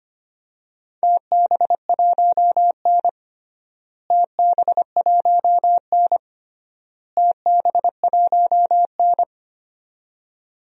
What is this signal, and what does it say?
Morse code, heard as a keyed tone, about 25 wpm, 700 Hz. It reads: TB1N TB1N TB1N
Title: DXCC-100 Call Signs - 4 Characters - Repeated 3x - 25wpm